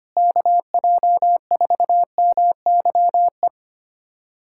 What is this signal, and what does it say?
Morse code, heard as a keyed tone, about 25 wpm, 700 Hz. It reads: KJ4MYE
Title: USA Callsign DE Callsign - 6 Characters - 25wpm